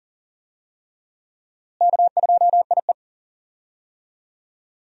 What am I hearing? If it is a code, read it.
Morse code, heard as a keyed tone, about 38 wpm, 700 Hz. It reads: K2IE
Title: USA Call Signs - 4 Characters - Repeated 3x - 40wpm